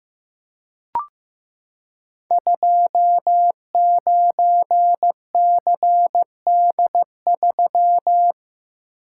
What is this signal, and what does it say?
Morse code, heard as a keyed tone, about 15 wpm, 700 Hz. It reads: 29CD3
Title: Code Groups - 5 Letter:Numbers - 15wpm